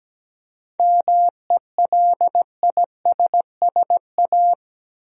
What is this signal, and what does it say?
Morse code, heard as a keyed tone, about 17 wpm, 700 Hz. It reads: MELISSA
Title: USA Common Names - 17wpm